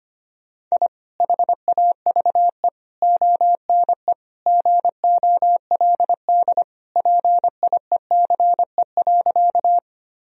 Morse code, heard as a keyed tone, about 25 wpm, 700 Hz. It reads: I HAVE ONE GOLD PIECE.
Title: Sentences from Top 1000 Words - Review Part 3 - 25wpm